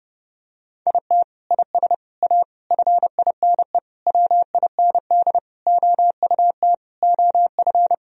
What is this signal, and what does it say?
Morse code, heard as a keyed tone, about 30 wpm, 700 Hz. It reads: IT IS A FINE WIND OUT OF
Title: Sentences from Top 1000 Words - Review Part 2 - 30wpm